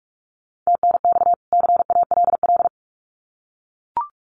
Morse code, heard as a keyed tone, about 44 wpm, 700 Hz. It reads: TNX CALL
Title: Common QSO Phrases 45wpm